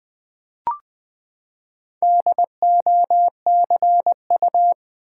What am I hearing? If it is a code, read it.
Morse code, heard as a keyed tone, about 20 wpm, 700 Hz. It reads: DOCU
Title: Build A Word - 4 Letters At A Time - 11-Letter Words - 20wpm